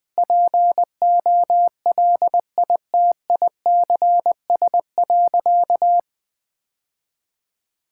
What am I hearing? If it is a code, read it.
Morse code, heard as a keyed tone, about 21 wpm, 700 Hz. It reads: POLITICS.
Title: Sentences from Top 1000 Words - Review Part 6 - 20wpm